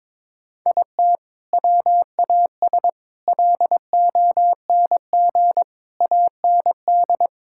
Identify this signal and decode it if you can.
Morse code, heard as a keyed tone, about 22 wpm, 700 Hz. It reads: IT WAS LONG AND
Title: Sentences from Top 1600 Words 22wpm